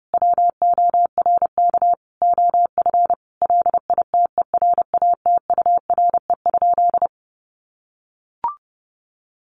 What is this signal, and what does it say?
Morse code, heard as a keyed tone, about 30 wpm, 700 Hz. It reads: WORK OF LITERATURE?
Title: Sentences from Top 600 Words 30wpm